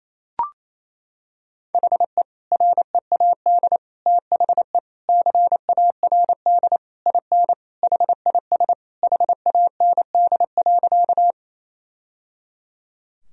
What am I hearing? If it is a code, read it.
Morse code, heard as a keyed tone, about 28 wpm, 700 Hz. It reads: HE READ THE CARD IN HIS HAND.
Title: Sentences from Top 1000 Words - Review Part 4 - 28wpm